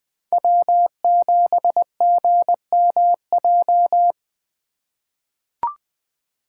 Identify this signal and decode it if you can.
Morse code, heard as a keyed tone, about 20 wpm, 700 Hz. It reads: W7GMJ
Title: USA Call Signs - 5 Characters - 4x Character Spacing - 20wpm